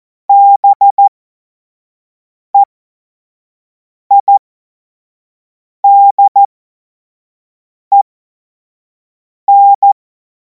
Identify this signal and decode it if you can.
Morse code, heard as a keyed tone, about 14 wpm, spaced as for 5 wpm, 800 Hz. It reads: BEIDEN
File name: Morsekurs / Witze / Witz 09. Juni 2026 25 F70